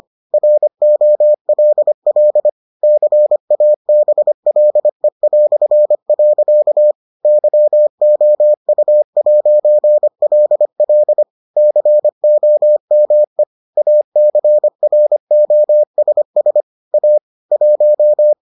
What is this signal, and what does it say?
Morse code, heard as a keyed tone, about 25 wpm, 600 Hz. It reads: ROLL CABLE". YOU'LL COME ACROSS A 1